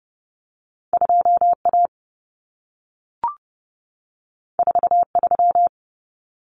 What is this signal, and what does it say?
Morse code, heard as a keyed tone, about 30 wpm, 700 Hz. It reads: 2A 43
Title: Common Cut Numbers - 2-Digit 30wpm